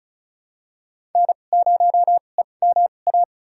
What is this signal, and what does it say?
Morse code, heard as a keyed tone, about 35 wpm, spaced as for 25 wpm, 700 Hz. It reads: N0EMA
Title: USA Call Signs - 5 Characters - 2x Character Spacing - 35wpm